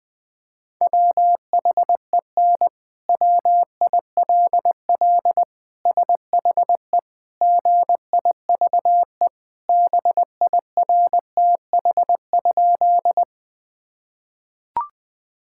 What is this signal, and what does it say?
Morse code, heard as a keyed tone, about 20 wpm, 700 Hz. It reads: WHEN WILL SHE GIVE BIRTH?
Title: Sentences from Top 1000 Words - Review Part 5 - 20wpm